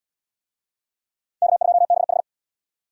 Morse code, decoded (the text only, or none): K2BH